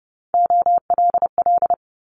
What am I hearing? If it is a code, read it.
Morse code, heard as a keyed tone, about 32 wpm, 700 Hz. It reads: OLL
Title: Multi-letter Phonemes Quick Fire 30wpm